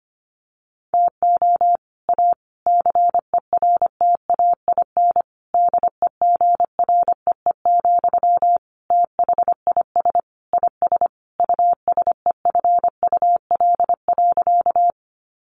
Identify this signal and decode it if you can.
Morse code, heard as a keyed tone, about 25 wpm, 700 Hz. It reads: TO A CERTAIN DEGREE, THIS IS USEFUL.
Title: Sentences from Top 1500 Words 25wpm